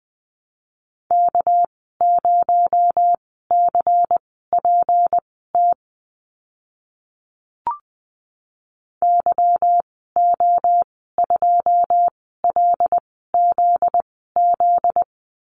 Morse code, heard as a keyed tone, about 19 wpm, spaced as for 14 wpm, 700 Hz. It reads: K0CPT YO2LZZ
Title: 245 Lesson 39g - Letter z - Call Signs 20wpm